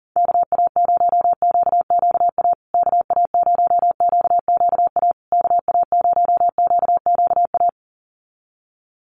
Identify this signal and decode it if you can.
Morse code, heard as a keyed tone, about 40 wpm, 700 Hz. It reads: KA0QQA KA0QQA KA0QQA
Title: USA Call Signs Repeated 3x 40wpm